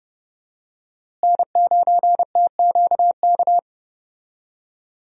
Morse code, heard as a keyed tone, about 30 wpm, 700 Hz. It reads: N9TQK
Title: USA Call Signs - 5 Characters - Repeated 3x - 30wpm